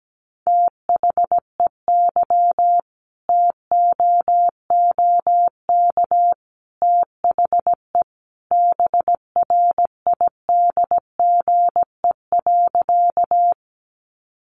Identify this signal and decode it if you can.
Morse code, heard as a keyed tone, about 17 wpm, 700 Hz. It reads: THEY TOOK THE BRIDGE.
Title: Sentences from Top 1200 Words 17wpm